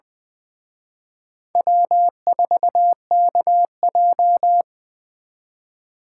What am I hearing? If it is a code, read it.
Morse code, heard as a keyed tone, about 23 wpm, 700 Hz. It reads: W4KJ
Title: Speed Racing - USA 1x2 Call Signs 20wpm